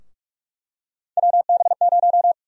Morse code, heard as a keyed tone, about 46 wpm, 700 Hz. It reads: WB0
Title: USA Call Sign Prefixes - 45wpm